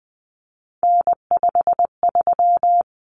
Morse code, heard as a keyed tone, about 20 wpm, 700 Hz. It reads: N53
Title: Common Cut Numbers - 3-Digit 20wpm